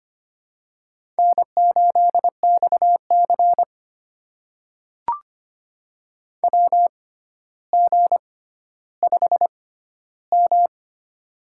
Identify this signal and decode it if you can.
Morse code, heard as a keyed tone, about 25 wpm, spaced as for 13 wpm, 700 Hz. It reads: N8XC WG5M
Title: USA Call Signs - 6x Character Spacing - 25wpm